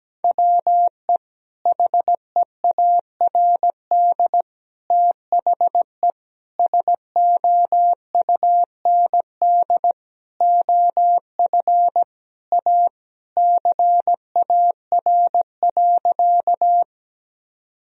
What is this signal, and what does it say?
Morse code, heard as a keyed tone, about 17 wpm, 700 Hz. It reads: WE HEARD THE SOUND OF A CAR.
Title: Sentences from Top 1000 Words - Review Part 1 - 17wpm